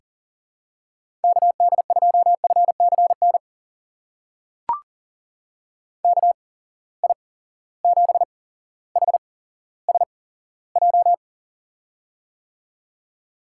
Morse code, heard as a keyed tone, about 40 wpm, spaced as for 17 wpm, 700 Hz. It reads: KD2FCN KI7HSJ
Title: USA Call Signs - 6 Characters - 8x Character Spacing - 40wpm